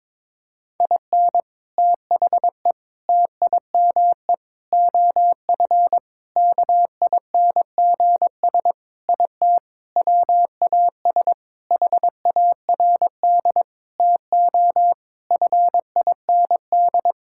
Morse code, heard as a keyed tone, about 22 wpm, 700 Hz. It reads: IN THE TIME OF KINGS IT WAS HARD TO FIND
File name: practice / 22wpm / Sentences with 3-Word Phrases - 22wpm